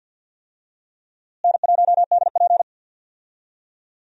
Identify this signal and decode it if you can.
Morse code, heard as a keyed tone, about 51 wpm, 700 Hz. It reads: N1DP
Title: USA Call Signs - 4 Characters - Repeated 2x - 50wpm